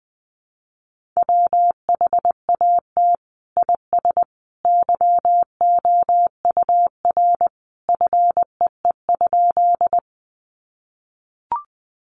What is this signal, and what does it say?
Morse code, heard as a keyed tone, about 20 wpm, 700 Hz. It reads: WHAT IS YOUR FEE?
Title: Sentences from Top 1500 Words 20wpm